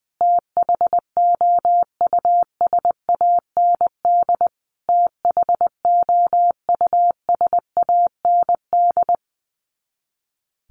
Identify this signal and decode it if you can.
Morse code, heard as a keyed tone, about 20 wpm, 700 Hz. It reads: THOUSAND THOUSAND
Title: Top 500 Words Repeated 20wpm